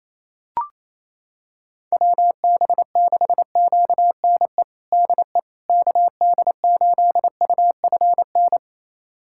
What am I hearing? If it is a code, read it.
Morse code, heard as a keyed tone, about 28 wpm, 700 Hz. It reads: WB6QNE DE KD8UFN
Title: USA Callsign DE Callsign - 6 Characters - 28wpm